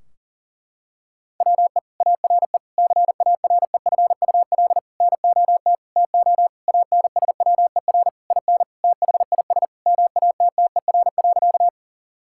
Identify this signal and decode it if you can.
Morse code, heard as a keyed tone, about 40 wpm, 700 Hz. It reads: WE ARE CAREFUL NOT TO ANSWER IN THIS MATTER.
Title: Sentences from Top 1600 Words 40wpm